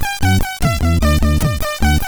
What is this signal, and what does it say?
There is no code.